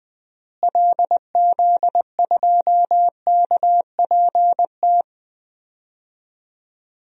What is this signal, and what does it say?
Morse code, heard as a keyed tone, about 20 wpm, 700 Hz. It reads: LZ2KPT
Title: DXCC-100 Call Signs - 6 Characters - Repeated 3x - 20wpm